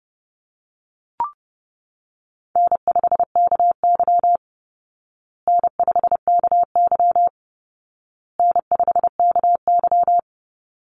Morse code, heard as a keyed tone, about 29 wpm, 700 Hz. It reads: N5KY N5KY N5KY
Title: Easy Call Signs Repeated 3x - 30wpm